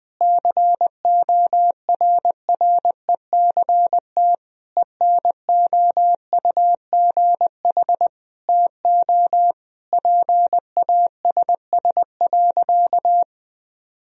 Morse code, heard as a keyed tone, about 20 wpm, 700 Hz. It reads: CORRECT ENOUGH TO PASS.
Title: Sentences from Top 1000 Words - Review Part 2 - 20wpm